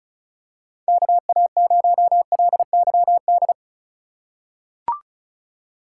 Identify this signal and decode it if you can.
Morse code, heard as a keyed tone, about 36 wpm, 700 Hz. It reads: KA0LYD
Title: USA Call Signs - 6 Characters - 4x Character Spacing - 35wpm